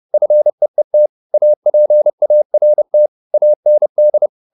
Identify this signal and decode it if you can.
Morse code, heard as a keyed tone, about 30 wpm, 600 Hz. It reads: FEET APART AND
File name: SS302 30